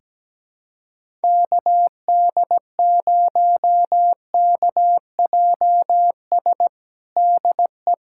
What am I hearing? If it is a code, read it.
Morse code, heard as a keyed tone, about 17 wpm, 700 Hz. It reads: KD0KJS DE
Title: USA Callsign DE Callsign - 6 Characters - 17wpm